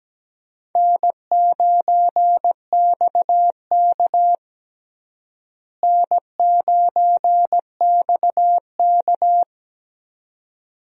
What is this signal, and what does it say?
Morse code, heard as a keyed tone, about 17 wpm, 700 Hz. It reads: N9XK N9XK
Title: USA Call Signs - 4 Characters - Repeated 2x - 17wpm